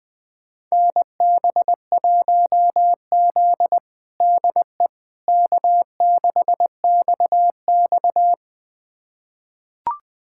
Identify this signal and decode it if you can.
Morse code, heard as a keyed tone, about 20 wpm, 700 Hz. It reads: NB1Z DE K6XX
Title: USA Callsign DE Callsign - 4 Characters - 20wpm